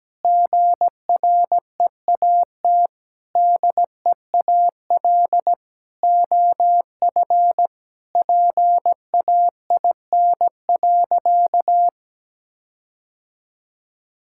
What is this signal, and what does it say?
Morse code, heard as a keyed tone, about 17 wpm, 700 Hz. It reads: GREAT DEAL OF PAIN.
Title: Sentences from Top 1600 Words 17wpm